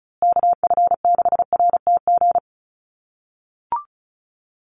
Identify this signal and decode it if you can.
Morse code, heard as a keyed tone, about 34 wpm, 700 Hz. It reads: KF6RTG
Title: USA Call Signs Repeated 3x 35wpm